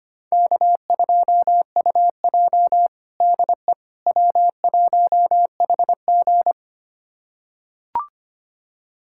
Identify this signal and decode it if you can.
Morse code, heard as a keyed tone, about 25 wpm, 700 Hz. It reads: K2UJ DE W1HG
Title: USA Callsign DE Callsign - 4 Characters - 25wpm